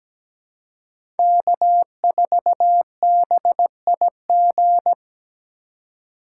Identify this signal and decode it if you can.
Morse code, heard as a keyed tone, about 17 wpm, 700 Hz. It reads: K4BIG
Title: USA Call Signs - 5 Characters - 4x Character Spacing - 17wpm